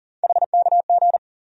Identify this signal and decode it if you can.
Morse code, heard as a keyed tone, about 40 wpm, 700 Hz. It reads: HKG
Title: USA Call Sign Postfixes - 3 Letters 40wpm